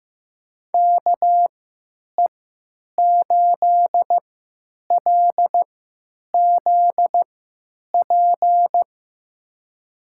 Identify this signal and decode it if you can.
Morse code, heard as a keyed tone, about 15 wpm, spaced as for 9 wpm, 700 Hz. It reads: KE8LZP
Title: USA Call Signs - 6 Characters - 3x Character Spacing - 15wpm